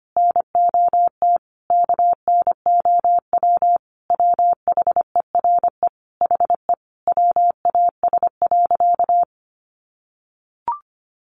Morse code, heard as a keyed tone, about 25 wpm, 700 Hz. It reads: NOT KNOW WHERE HE WAS.
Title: Sentences from Top 600 Words 25wpm